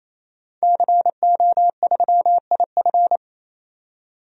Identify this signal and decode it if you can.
Morse code, heard as a keyed tone, about 28 wpm, 700 Hz. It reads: CO3IF
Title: DXCC-100 Call Signs - 5 Characters - 28wpm